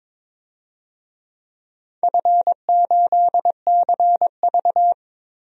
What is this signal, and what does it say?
Morse code, heard as a keyed tone, about 22 wpm, 700 Hz. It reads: F8CV
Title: DXCC-100 Call Signs - 4 Characters - 22wpm